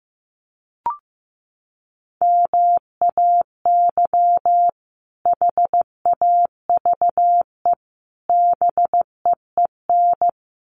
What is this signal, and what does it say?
Morse code, heard as a keyed tone, about 15 wpm, 700 Hz. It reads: MAY HAVE BEEN
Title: Sets of 3 Words 15wpm